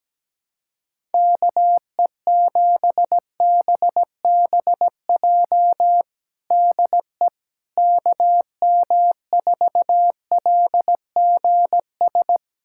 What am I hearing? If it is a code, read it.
Morse code, heard as a keyed tone, about 17 wpm, 700 Hz. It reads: KE7BBJ DE KM4LGS
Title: USA Callsign DE Callsign 17wpm